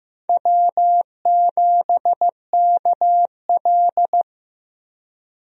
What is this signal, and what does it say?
Morse code, heard as a keyed tone, about 15 wpm, 700 Hz. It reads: W7KL
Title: USA Call Signs - 4 Characters - Repeated 2x - 15wpm